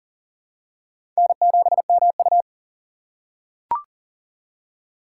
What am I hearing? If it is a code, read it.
Morse code, heard as a keyed tone, about 42 wpm, 700 Hz. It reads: N7MU
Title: USA Call Signs - 4 Characters - 3x Character Spacing - 40wpm